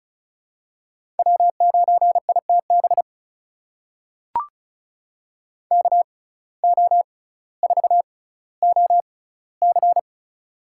Morse code, heard as a keyed tone, about 36 wpm, spaced as for 18 wpm, 700 Hz. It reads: W9ITB KO4OC
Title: USA Call Signs - 6x Character Spacing - 35wpm